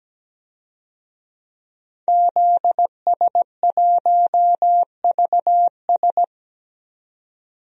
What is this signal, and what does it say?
Morse code, heard as a keyed tone, about 17 wpm, 700 Hz. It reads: ZS1VS
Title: DXCC-100 Call Signs - 5 Characters - 17wpm